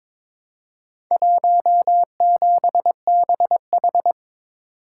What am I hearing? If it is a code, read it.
Morse code, heard as a keyed tone, about 23 wpm, 690 Hz. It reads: 17BH
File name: practice / 22wpm / Four Letter-Number Combinations - 22wpm